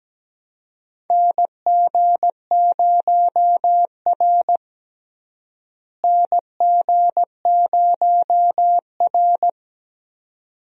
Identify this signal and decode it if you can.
Morse code, heard as a keyed tone, about 17 wpm, 700 Hz. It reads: NG0R NG0R
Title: USA Call Signs - 4 Characters - Repeated 2x - 17wpm